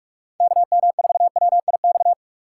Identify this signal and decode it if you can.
Morse code, heard as a keyed tone, about 43 wpm, 700 Hz. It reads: KM4WIX
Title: USA Call Signs - 6 Characters - 3x Character Spacing - 45wpm